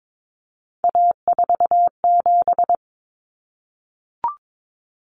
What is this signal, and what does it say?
Morse code, heard as a keyed tone, about 22 wpm, 700 Hz. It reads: A47
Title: Common Cut Numbers - 3-Digit 22wpm